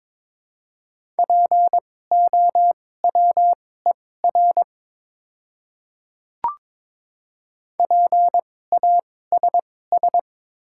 Morse code, heard as a keyed tone, about 22 wpm, spaced as for 16 wpm, 690 Hz. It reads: POWER PASS
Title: 2x Character Spacing - Top 500 Words - 22wpm